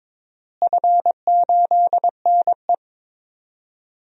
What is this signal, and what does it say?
Morse code, heard as a keyed tone, about 22 wpm, 700 Hz. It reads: F8NE
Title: DXCC-100 Call Signs - 4 Characters - Repeated 2x - 22wpm